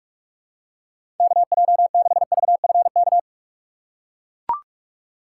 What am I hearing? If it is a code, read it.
Morse code, heard as a keyed tone, about 42 wpm, 700 Hz. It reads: KJ6VFK
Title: USA Call Signs - 6 Characters - 3x Character Spacing - 45wpm